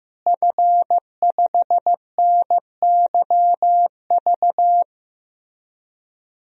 Morse code, heard as a keyed tone, about 15 wpm, 700 Hz. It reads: F5NYV
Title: DXCC-100 Call Signs - 5 Characters - Repeated 3x - 15wpm